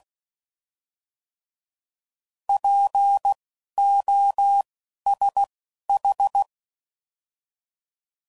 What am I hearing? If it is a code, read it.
Morse code, heard as a keyed tone, about 16 wpm, spaced as for 11 wpm, 800 Hz. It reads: POSH
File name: Prefixes / By First Letter / Ps